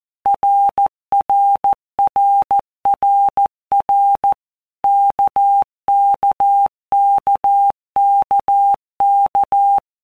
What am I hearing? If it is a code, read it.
Morse code, heard as a keyed tone, about 14 wpm, 800 Hz. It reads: RRRRR KKKKK
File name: All Groups 1-minute Blocks / Groups@13 / RKPX